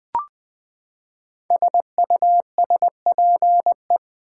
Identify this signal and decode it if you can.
Morse code, heard as a keyed tone, about 20 wpm, 700 Hz. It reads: SUSPE
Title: Build A Word - 5 Letters At A Time - 10-Letter Words - 20wpm